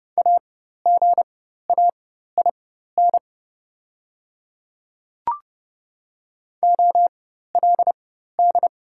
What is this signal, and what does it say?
Morse code, heard as a keyed tone, about 30 wpm, spaced as for 14 wpm, 700 Hz. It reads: AGAIN OLD
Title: 4x Character Spacing - Top 200 Words - 30wpm